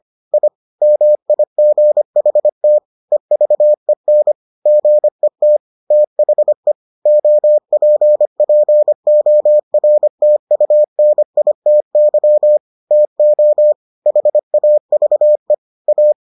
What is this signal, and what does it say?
Morse code, heard as a keyed tone, about 25 wpm, 600 Hz. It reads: I MIGHT EVEN GET THE OPPORTUNITY TO HAVE A